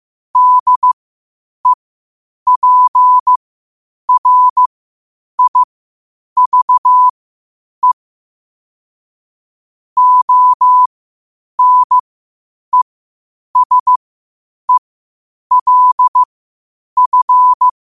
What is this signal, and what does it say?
Morse code, heard as a keyed tone, about 15 wpm, spaced as for 8 wpm, 1000 Hz. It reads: DEPRIVE ONESELF
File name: 8wpm